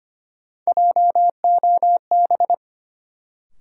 Morse code, heard as a keyed tone, about 25 wpm, 700 Hz. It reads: JOB
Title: Sentences from Top 600 Words - New Word Drill - 25wpm